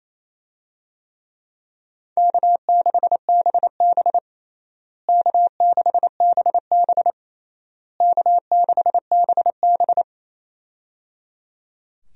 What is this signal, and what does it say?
Morse code, heard as a keyed tone, about 28 wpm, 700 Hz. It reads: K6BB K6BB K6BB
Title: USA Call Signs - 4 Characters - Repeated 3x - 28wpm